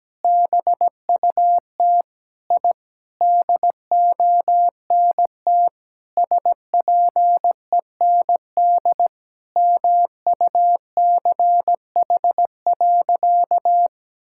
Morse code, heard as a keyed tone, about 17 wpm, 700 Hz. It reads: BUT I DONT SPEND MUCH.
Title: Sentences from Top 1000 Words - Review Part 5 - 17wpm